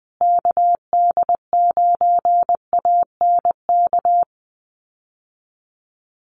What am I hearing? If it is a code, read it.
Morse code, heard as a keyed tone, about 20 wpm, 700 Hz. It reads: KD9ANK
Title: USA Call Signs Repeated 3x 20wpm